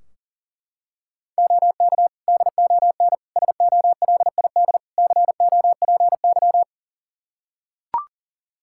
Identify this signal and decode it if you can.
Morse code, heard as a keyed tone, about 40 wpm, 700 Hz. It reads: OK DON SOLID COPY
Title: Practice QSO - Exchange 3 - More Info - 40wpm